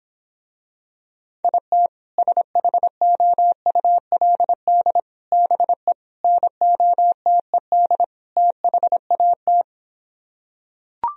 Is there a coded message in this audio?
Yes: Morse code from 1.4 to 9.6 s, heard as a keyed tone, about 26 wpm, 700 Hz. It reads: IT SHOULD BE NOTED THAT